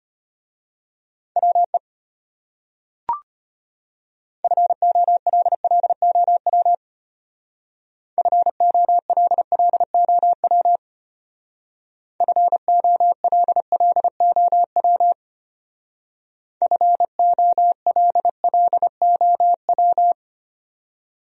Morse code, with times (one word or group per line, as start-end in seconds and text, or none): 1.4-1.8 WE
4.4-6.7 FOLLOW
8.2-10.8 FOLLOW
12.2-15.1 FOLLOW
16.6-20.1 FOLLOW